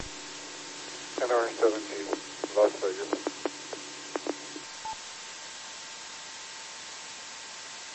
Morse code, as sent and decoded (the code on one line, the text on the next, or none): none